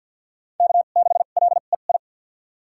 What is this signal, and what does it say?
Morse code, heard as a keyed tone, about 52 wpm, spaced as for 36 wpm, 700 Hz. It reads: K6LEI
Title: USA Call Signs - 2x Character Spacing - 50wpm